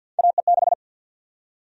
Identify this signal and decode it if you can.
Morse code, heard as a keyed tone, about 52 wpm, 700 Hz. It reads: AE6
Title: USA Call Sign Prefixes - Repeated 2x - 50wpm